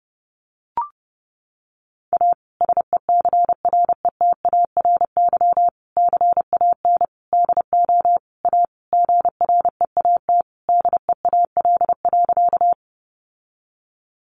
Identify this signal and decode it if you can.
Morse code, heard as a keyed tone, about 30 wpm, 700 Hz. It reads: A SECRETARY CAN DO A GREAT DEAL.
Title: Sentences from Top 1300 Words 30wpm